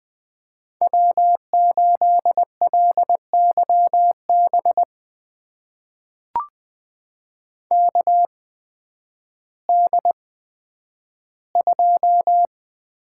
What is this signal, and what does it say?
Morse code, heard as a keyed tone, about 20 wpm, spaced as for 11 wpm, 700 Hz. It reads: W8LYB KD2
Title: USA Call Signs - 8x Character Spacing - 20wpm